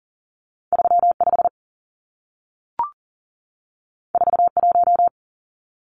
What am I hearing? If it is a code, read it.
Morse code, heard as a keyed tone, about 40 wpm, 700 Hz. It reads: 35 41
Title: Numbers - 2-Digit 40wpm